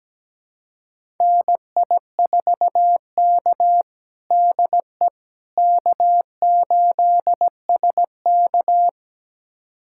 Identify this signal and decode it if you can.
Morse code, heard as a keyed tone, about 17 wpm, 700 Hz. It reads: NI4K DE K8SK
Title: USA Callsign DE Callsign - 4 Characters - 17wpm